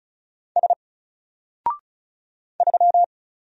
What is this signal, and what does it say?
Morse code, heard as a keyed tone, about 35 wpm, 700 Hz. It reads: S 3
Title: Single Letter-Number 35wpm